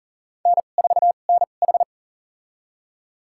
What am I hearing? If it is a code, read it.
Morse code, heard as a keyed tone, about 38 wpm, spaced as for 28 wpm, 700 Hz. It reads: N4NH
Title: USA Call Signs - 4 Characters - 2x Character Spacing - 40wpm